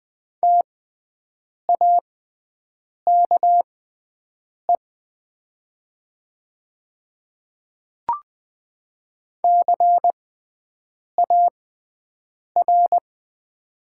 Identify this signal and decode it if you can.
Morse code, heard as a keyed tone, about 20 wpm, spaced as for 7 wpm, 700 Hz. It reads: TAKE CAR
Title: 6x Character Spacing - Top 300 Words - 20wpm